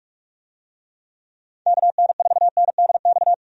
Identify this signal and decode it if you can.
Morse code, heard as a keyed tone, about 46 wpm, 700 Hz. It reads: KN4NDX
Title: USA Call Signs - 6 Characters - 45wpm